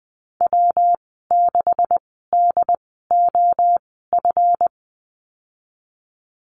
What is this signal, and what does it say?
Morse code, heard as a keyed tone, about 20 wpm, spaced as for 14 wpm, 700 Hz. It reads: W6DOF
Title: 154 Lesson 25g - Letter p - Call Signs 20wpm